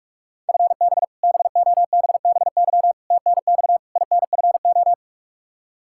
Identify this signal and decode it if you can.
Morse code, heard as a keyed tone, about 46 wpm, 700 Hz. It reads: FB BOBBY TNX INFO
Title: Practice QSO - Exchange 3 - More Info - 45wpm